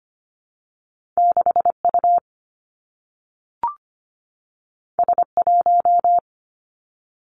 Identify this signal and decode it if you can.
Morse code, heard as a keyed tone, about 25 wpm, 700 Hz. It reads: 6U S1
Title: Two Challenging Letter-Number Combinations 25wpm